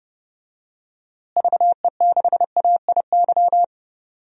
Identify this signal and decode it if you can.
Morse code, heard as a keyed tone, about 31 wpm, 700 Hz. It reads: VE6AIY
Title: DXCC-100 Call Signs - 6 Characters - Repeated 3x - 30wpm